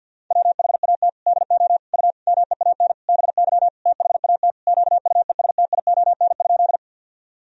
Morse code, heard as a keyed tone, about 50 wpm, 700 Hz. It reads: WHAT DO U MEAN BY THAT QUESTION?
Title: Sentences from Top 400 Words and Abbreviations 50wpm